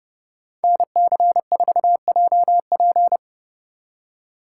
Morse code, heard as a keyed tone, about 29 wpm, 700 Hz. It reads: NC4JP